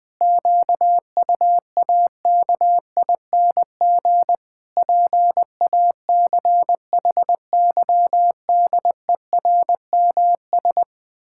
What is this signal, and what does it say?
Morse code, heard as a keyed tone, about 20 wpm, 700 Hz. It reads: QUAKING PACHYDERMS
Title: Pangram Challenge v2 - 20wpm